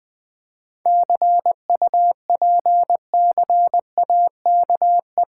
Morse code, heard as a keyed tone, about 21 wpm, 690 Hz. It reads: CUPCAKE
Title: Difficult Words with Hints - 20wpm